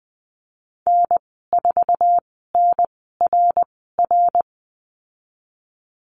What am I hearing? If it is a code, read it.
Morse code, heard as a keyed tone, about 20 wpm, spaced as for 14 wpm, 700 Hz. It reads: N4NRR
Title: 140 Lesson 23g - Letter f - Call Signs 20wpm